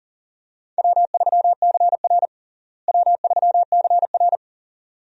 Morse code, heard as a keyed tone, about 40 wpm, 700 Hz. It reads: W3CR W3CR
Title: USA Call Signs - 4 Characters - Repeated 2x - 40wpm